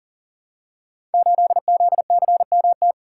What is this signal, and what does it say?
Morse code, heard as a keyed tone, about 42 wpm, 700 Hz. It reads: 8ZCMT
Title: Code Groups - 5 Letter:Numbers - 40wpm